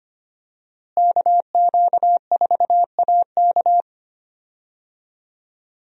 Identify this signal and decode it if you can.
Morse code, heard as a keyed tone, about 25 wpm, 700 Hz. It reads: KQ4AK
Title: USA Call Signs - 5 Characters - Repeated 2x - 25wpm